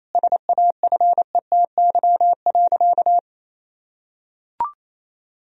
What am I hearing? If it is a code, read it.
Morse code, heard as a keyed tone, about 28 wpm, 700 Hz. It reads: SAFETY.
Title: Sentences from Top 1000 Words - Review Part 4 - 28wpm